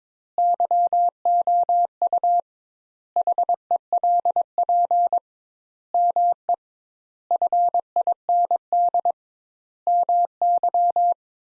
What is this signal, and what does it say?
Morse code, heard as a keyed tone, about 22 wpm, 700 Hz. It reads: YOU HELP ME FIND MY
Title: Sentences Top 200 Words - 22wpm